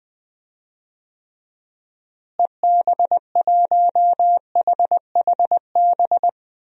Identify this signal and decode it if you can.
Morse code, heard as a keyed tone, about 20 wpm, 700 Hz. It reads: EB1HHB
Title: DXCC-100 Call Signs - 6 Characters - Repeated 2x - 20wpm